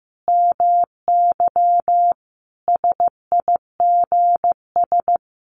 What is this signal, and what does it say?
Morse code, heard as a keyed tone, about 15 wpm, 700 Hz. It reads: MY SIGS
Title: Common QSO Phrases 15wpm